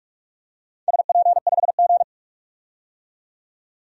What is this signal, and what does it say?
Morse code, heard as a keyed tone, about 45 wpm, 700 Hz. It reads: SP5G
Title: DXCC-100 Call Signs - 4 Characters - 45wpm